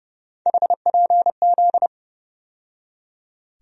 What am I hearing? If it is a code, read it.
Morse code, heard as a keyed tone, about 30 wpm, 700 Hz. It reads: HPZ
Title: USA Call Sign Postfixes - 3 Letters 30wpm